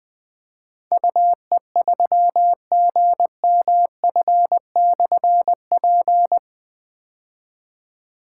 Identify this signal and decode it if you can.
Morse code, heard as a keyed tone, about 20 wpm, 700 Hz. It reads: UE3GMF/P
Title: Difficult Call Signs 20wpm